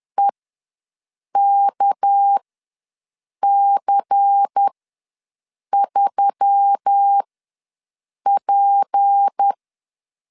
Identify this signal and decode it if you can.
Morse code, heard as a keyed tone, about 10 wpm, spaced as for 6 wpm, 800 Hz. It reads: EKC3P
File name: LEZIONI / 6 PKX0 / 6 PK ws 10wpm 3745